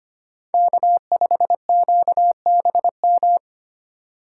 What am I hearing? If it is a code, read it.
Morse code, heard as a keyed tone, about 25 wpm, 700 Hz. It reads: K5QBM
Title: USA Call Signs - 5 Characters - 4x Character Spacing - 25wpm